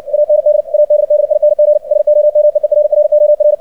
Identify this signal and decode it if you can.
Morse code, heard as a keyed tone, about 60 wpm, 600 Hz. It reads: AND ADDING SOME ERROR
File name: archived / CRSnet